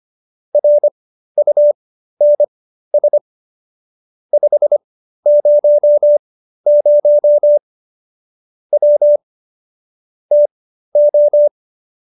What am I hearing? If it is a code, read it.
Morse code, heard as a keyed tone, about 25 wpm, spaced as for 13 wpm, 590 Hz. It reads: RUNS 500 W TO